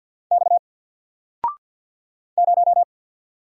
Morse code, heard as a keyed tone, about 50 wpm, 700 Hz. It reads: X 0
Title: Single Character (Letter:Number:Punctuation:Prosign) 50wpm